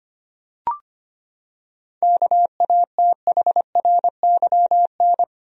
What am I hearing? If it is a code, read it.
Morse code, heard as a keyed tone, about 25 wpm, 700 Hz. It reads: KATHRYN
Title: USA Common Names - 25wpm